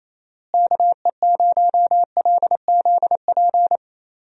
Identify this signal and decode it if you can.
Morse code, heard as a keyed tone, about 28 wpm, 700 Hz. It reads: KE0LZP
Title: USA Call Signs - 6 Characters - Repeated 3x - 28wpm